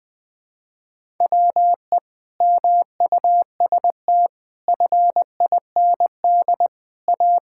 Morse code, heard as a keyed tone, about 20 wpm, 700 Hz. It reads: WE MUST FIND A
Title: Sentences with 5-Word Phrases - Spoken Beginning - 20wpm